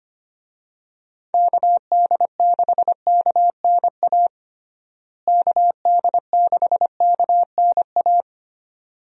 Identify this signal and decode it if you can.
Morse code, heard as a keyed tone, about 25 wpm, 700 Hz. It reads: KD6KNA KD6KNA
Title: USA Call Signs - 6 Characters - Repeated 2x - 25wpm